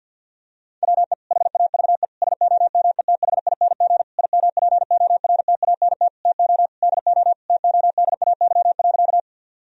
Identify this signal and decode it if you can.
Morse code, heard as a keyed tone, about 50 wpm, 700 Hz. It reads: WE HAVE SOMETHING IMPORTANT TO DO TODAY.
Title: Sentences from Top 1000 Words - Review Part 4 - 50wpm